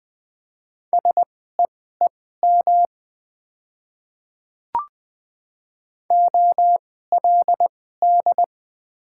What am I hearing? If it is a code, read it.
Morse code, heard as a keyed tone, about 20 wpm, spaced as for 14 wpm, 700 Hz. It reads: SEEM OLD
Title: 2x Character Spacing - Top 300 Words - 20wpm